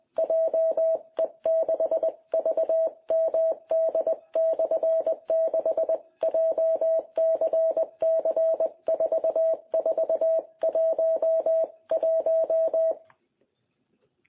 Morse code, heard as a keyed tone, about 20 wpm, 660 Hz. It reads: JE6VMD/6 JCC4411
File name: jh8cbh / 2021cw